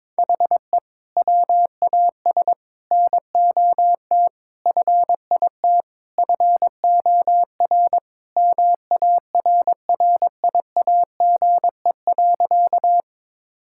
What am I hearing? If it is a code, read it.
Morse code, heard as a keyed tone, about 22 wpm, 700 Hz. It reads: HE WAS NOT FIT FOR MARRIAGE.
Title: Sentences from Top 1000 Words - Review Part 5 - 22wpm